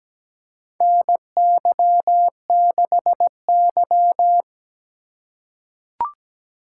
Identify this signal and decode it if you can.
Morse code, heard as a keyed tone, about 17 wpm, 700 Hz. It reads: NY6Y
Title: K1USN SST QSO Practice - 17wpm